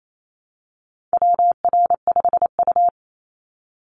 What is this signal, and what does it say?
Morse code, heard as a keyed tone, about 28 wpm, 700 Hz. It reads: WR5U
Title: Call Signs Repeated 2x - 28wpm